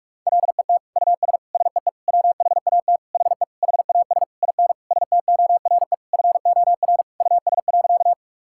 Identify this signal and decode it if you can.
Morse code, heard as a keyed tone, about 46 wpm, 700 Hz. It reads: LET US SEE WHAT HE HAS IN STORE FOR US.
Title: Sentences from Top 1000 Words - Review Part 5 - 45wpm